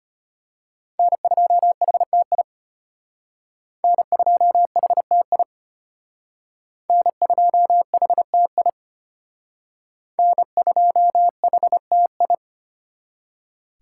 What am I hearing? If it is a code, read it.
Morse code, heard as a keyed tone, about 34 wpm, 700 Hz. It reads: N2HTI N2HTI N2HTI N2HTI